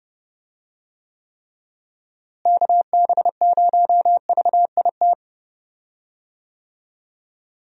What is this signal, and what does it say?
Morse code, heard as a keyed tone, about 30 wpm, 700 Hz. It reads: KB0VIT
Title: USA Call Signs - 6 Characters - Repeated 3x - 30wpm